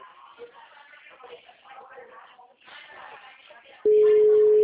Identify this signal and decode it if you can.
Ringback tone (Europe, 425 Hz), with speech.